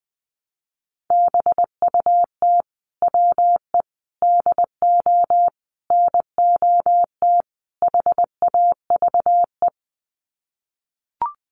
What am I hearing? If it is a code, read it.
Morse code, heard as a keyed tone, about 20 wpm, 700 Hz. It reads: BUT WE DO NOT HAVE
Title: Sets of 5 Words Encore2 20wpm